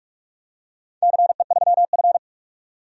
Morse code, heard as a keyed tone, about 44 wpm, 700 Hz. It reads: CE3F
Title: DXCC-100 Call Signs - 4 Characters - 45wpm